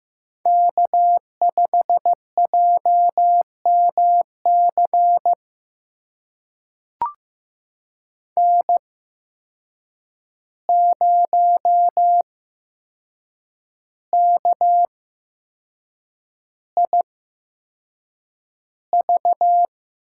Morse code, heard as a keyed tone, about 15 wpm, spaced as for 6 wpm, 700 Hz. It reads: K5JMC N0KIV
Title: USA Call Signs - 8x Character Spacing - 15wpm